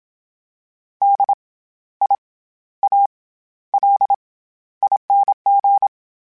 Morse code, heard as a keyed tone, about 27 wpm, 800 Hz. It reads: D I A L ING
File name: Special INGs-mp3s / Special INGs-mp3s / SlowToFaster ing @25